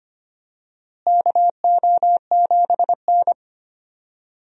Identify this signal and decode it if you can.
Morse code, heard as a keyed tone, about 25 wpm, 700 Hz. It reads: KO7N
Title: USA Call Signs - 4 Characters - Repeated 3x - 25wpm